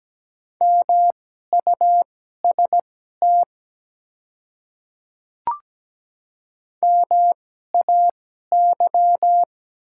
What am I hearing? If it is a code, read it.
Morse code, heard as a keyed tone, about 17 wpm, spaced as for 12 wpm, 700 Hz. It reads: MUST MAY
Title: 2x Character Spacing - Top 200 Words - 17wpm